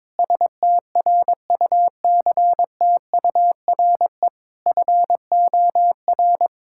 Morse code, heard as a keyed tone, about 22 wpm, 700 Hz. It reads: STRUCTURE FOR
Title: Sentences from Top 1000 Words - Review Part 6 - 22wpm